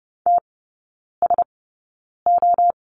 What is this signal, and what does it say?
Morse code, heard as a keyed tone, about 30 wpm, spaced as for 9 wpm, 690 Hz. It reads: TSO
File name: Lesson 16e - Letter c - Three Characters 30wpm